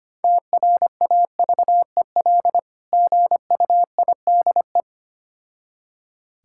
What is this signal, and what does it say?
Morse code, heard as a keyed tone, about 25 wpm, 700 Hz. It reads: TRAVEL GUIDE
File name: practice / 25wpm / Difficult Words with Hints - 25wpm